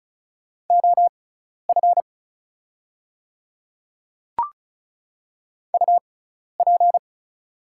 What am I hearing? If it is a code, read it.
Morse code, heard as a keyed tone, about 35 wpm, spaced as for 12 wpm, 690 Hz. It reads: OF UP
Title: 6x Character Spacing - Top 100 Words - 35wpm